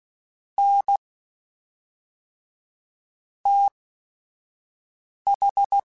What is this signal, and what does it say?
Morse code, heard as a keyed tone, about 16 wpm, spaced as for 4 wpm, 800 Hz. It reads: NTH
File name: Technical Subjects / Technical Subjects-mp3s / Sw Pwr Sup @5